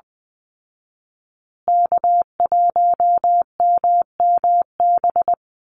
Morse code, heard as a keyed tone, about 20 wpm, 700 Hz. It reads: K1MMB
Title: Call Signs Repeated 2x - 20wpm